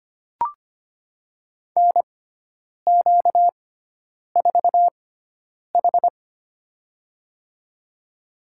Morse code, heard as a keyed tone, about 25 wpm, spaced as for 9 wpm, 700 Hz. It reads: NQ4H
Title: USA Call Signs - 4 Characters - 6x Character Spacing - 25wpm